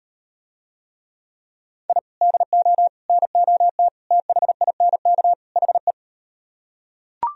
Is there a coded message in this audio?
Yes: Morse code from 1.9 to 5.9 s, heard as a keyed tone, about 38 wpm, 700 Hz. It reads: I DO NOT THINK HE